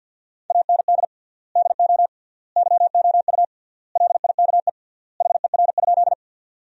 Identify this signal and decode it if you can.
Morse code, heard as a keyed tone, about 50 wpm, 700 Hz. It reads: AND DO YOU LIKE HER?
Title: 3x Word-Spacing - Sentences from Top 100 Words - 50wpm